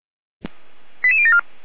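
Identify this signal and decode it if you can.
Five-tone selective call (ZVEI), heard as quick runs of five tones, tones of about 70 ms. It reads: 88074
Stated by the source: ZVEI five-tone call 88247 received on SDR, from audio.f85360.feuerwehr-orlamuende.org (Feuerwehr Orlamünde fire brigade).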